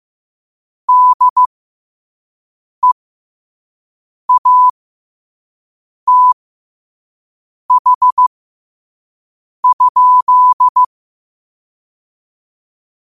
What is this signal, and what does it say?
Morse code, heard as a keyed tone, about 15 wpm, spaced as for 5 wpm, 1000 Hz. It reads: DEATH?